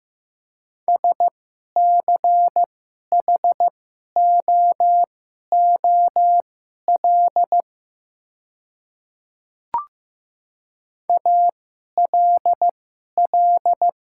Morse code, heard as a keyed tone, about 15 wpm, spaced as for 11 wpm, 700 Hz. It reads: SCHOOL ALL
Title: 2x Character Spacing - Top 300 Words - 15wpm